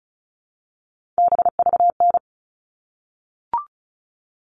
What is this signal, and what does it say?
Morse code, heard as a keyed tone, about 34 wpm, 700 Hz. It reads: BVN
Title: All Cut Numbers - 3-Digit 35wpm